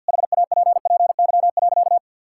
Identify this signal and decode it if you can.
Morse code, heard as a keyed tone, about 50 wpm, 700 Hz. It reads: HAPPY.
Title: Sentences from Top 1000 Words - Review Part 5 - 50wpm